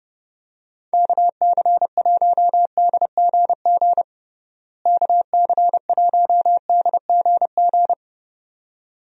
Morse code, heard as a keyed tone, about 30 wpm, 700 Hz. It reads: KC1DGG KC1DGG
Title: USA Call Signs - 6 Characters - Repeated 2x - 30wpm